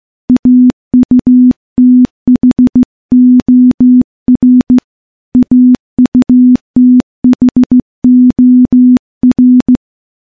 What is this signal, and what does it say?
Morse code, heard as a keyed tone, about 15 wpm, 250 Hz. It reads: AUTHOR AUTHOR
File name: morse-training-15WPM